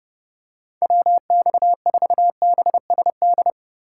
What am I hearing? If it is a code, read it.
Morse code, heard as a keyed tone, about 31 wpm, 700 Hz. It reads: WX4BSD